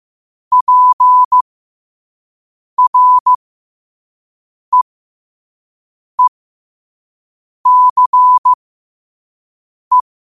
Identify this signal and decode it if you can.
Morse code, heard as a keyed tone, about 15 wpm, spaced as for 5 wpm, 1000 Hz. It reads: PREECE
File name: qotdfiles / 5wpm